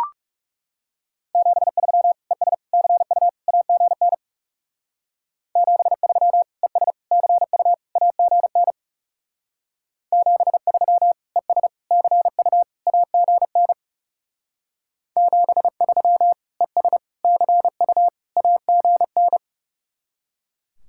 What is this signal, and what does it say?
Morse code, heard as a keyed tone, about 36 wpm, 700 Hz. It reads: 73 ES CU AGN 73 ES CU AGN 73 ES CU AGN 73 ES CU AGN
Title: Speed-Racing Common QSO Phrases - 30wpm